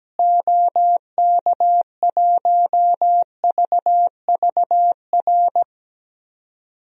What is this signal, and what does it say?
Morse code, heard as a keyed tone, about 17 wpm, 700 Hz. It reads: OK1VVR
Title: DXCC-100 Call Signs - 6 Characters - Repeated 3x - 17wpm